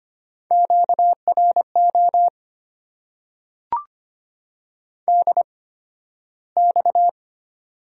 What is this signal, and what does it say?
Morse code, heard as a keyed tone, about 25 wpm, spaced as for 13 wpm, 690 Hz. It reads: QRO DX
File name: Top 90 QSO Elements - 8x Character Spacing - 25wpm